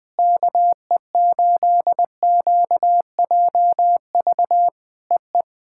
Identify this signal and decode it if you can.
Morse code, heard as a keyed tone, about 20 wpm, 700 Hz. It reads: KE8QJV EE
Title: POTA QSO Practice - 20wpm